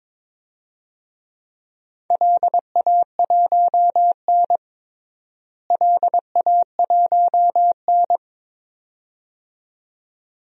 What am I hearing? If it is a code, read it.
Morse code, heard as a keyed tone, about 22 wpm, 700 Hz. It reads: LA1N LA1N
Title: DXCC-100 Call Signs - 4 Characters - Repeated 2x - 22wpm